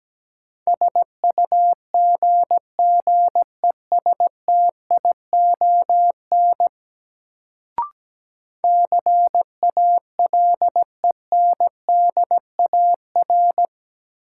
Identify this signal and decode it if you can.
Morse code, heard as a keyed tone, about 17 wpm, 700 Hz. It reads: SUGGESTION CALENDAR
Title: Sentences from Top 1600 Words - New Word Drill - 17wpm